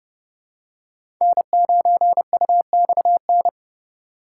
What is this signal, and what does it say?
Morse code, heard as a keyed tone, about 30 wpm, 700 Hz. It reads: N9UXN
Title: USA Call Signs - 5 Characters - 3x Character Spacing - 30wpm